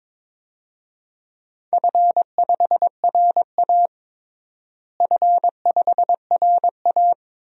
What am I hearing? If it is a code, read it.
Morse code, heard as a keyed tone, about 22 wpm, 700 Hz. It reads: F5RA F5RA
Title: DXCC-100 Call Signs - 4 Characters - Repeated 2x - 22wpm